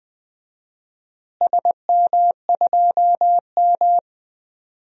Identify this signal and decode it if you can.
Morse code, heard as a keyed tone, about 20 wpm, 700 Hz. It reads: SM2M
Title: DXCC-100 Call Signs - 4 Characters - Repeated 2x - 20wpm